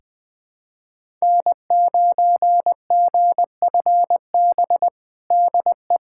Morse code, heard as a keyed tone, about 20 wpm, 700 Hz. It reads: N9GFB DE
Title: USA Callsign DE Callsign - 5 Characters - 20wpm